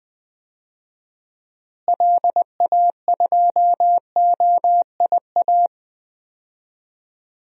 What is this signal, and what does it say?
Morse code, heard as a keyed tone, about 20 wpm, 700 Hz. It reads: LA2OIA
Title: DXCC-100 Call Signs - 6 Characters - Repeated 2x - 20wpm